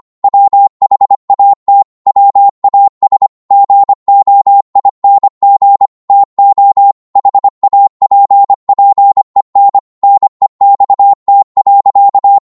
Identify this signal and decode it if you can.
Morse code, heard as a keyed tone, about 25 wpm, 800 Hz. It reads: WHAT WAS GOING TO HAPPEN NEXT.